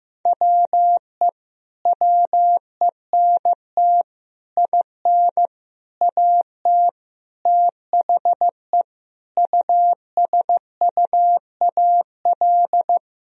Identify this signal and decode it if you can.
Morse code, heard as a keyed tone, about 15 wpm, 700 Hz. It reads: WE WENT IN AT THE USUAL
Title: Sentences from Top 1000 Words - Review Part 2 - 15wpm